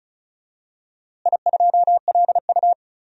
Morse code, heard as a keyed tone, about 36 wpm, 700 Hz. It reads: I2LU
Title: DXCC-100 Call Signs - 4 Characters - Repeated 2x - 35wpm